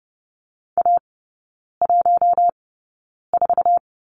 Morse code, heard as a keyed tone, about 30 wpm, spaced as for 9 wpm, 700 Hz. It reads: A14
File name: Lesson 18e - Letter m - Three Characters 30wpm